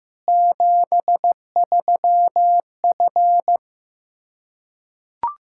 Morse code, heard as a keyed tone, about 15 wpm, 700 Hz. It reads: 73F
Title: Practice QSO - Exchange 2 - Info - 15wpm